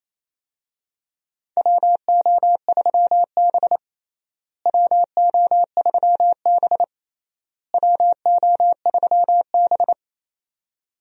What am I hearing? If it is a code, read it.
Morse code, heard as a keyed tone, about 28 wpm, 700 Hz. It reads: WO3B WO3B WO3B
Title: USA Call Signs - 4 Characters - Repeated 3x - 28wpm